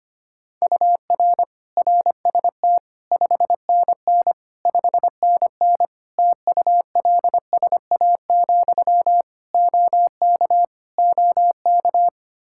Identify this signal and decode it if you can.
Morse code, heard as a keyed tone, about 25 wpm, 700 Hz. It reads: UR RST 5NN 5NN TULSA, OK OK
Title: Practice QSO - Exchange 1 - Beginning - 25wpm